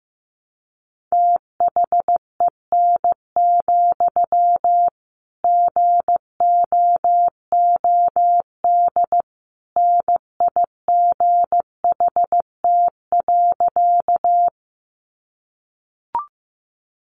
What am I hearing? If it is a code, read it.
Morse code, heard as a keyed tone, about 15 wpm, 700 Hz. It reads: THEN, GOOD NIGHT.
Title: Sentences from Top 300 Words 15wpm